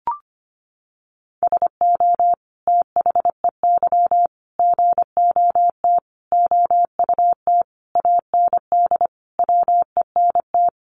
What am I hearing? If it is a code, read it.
Morse code, heard as a keyed tone, about 25 wpm, 700 Hz. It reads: SO THEY GOT OUT AND WENT
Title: Sentences from Top 300 Words 25wpm